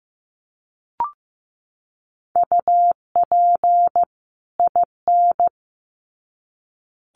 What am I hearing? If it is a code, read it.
Morse code, heard as a keyed tone, about 15 wpm, 700 Hz. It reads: UP IN
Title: Sets of 2 Words 15wpm